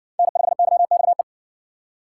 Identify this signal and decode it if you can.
Morse code, heard as a keyed tone, about 60 wpm, 700 Hz. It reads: N5YXE